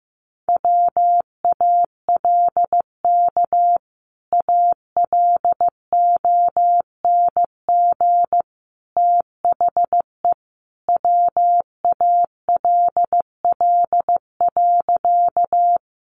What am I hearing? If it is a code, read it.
Morse code, heard as a keyed tone, about 15 wpm, 700 Hz. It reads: WALK ALONG THE WALL.
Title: Sentences from Top 1100 Words 15wpm